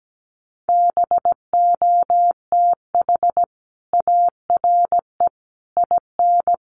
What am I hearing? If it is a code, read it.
Morse code, heard as a keyed tone, about 17 wpm, 700 Hz. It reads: BOTH ARE IN
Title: Sentences from Top 1200 Words 17wpm